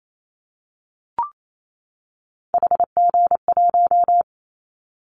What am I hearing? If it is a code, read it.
Morse code, heard as a keyed tone, about 28 wpm, 700 Hz. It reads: HG1
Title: Callsign Prefixes for Top 25 DXCC - 28wpm